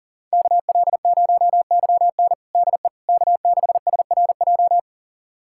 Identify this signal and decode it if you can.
Morse code, heard as a keyed tone, about 40 wpm, 700 Hz. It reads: KL0YN DE K6SRJ
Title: USA Callsign DE Callsign - 5 Characters - 40wpm